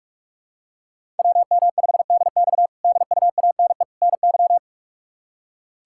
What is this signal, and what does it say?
Morse code, heard as a keyed tone, about 42 wpm, 700 Hz. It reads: WM5DX DUANE NY
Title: K1USN SST QSO Practice - 45wpm